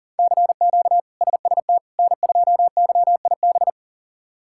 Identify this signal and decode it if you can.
Morse code, heard as a keyed tone, about 40 wpm, 700 Hz. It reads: CQ SST N2YIB
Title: K1USN SST QSO Practice - 40wpm